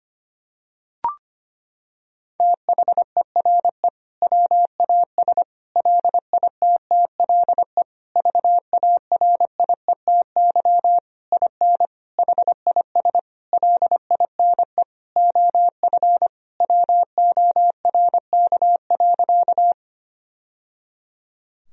Morse code, heard as a keyed tone, about 25 wpm, 700 Hz. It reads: THERE WAS LITTLE VARIETY IN HIS LINE OF WORK.
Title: Sentences from Top 1000 Words - Review Part 4 - 25wpm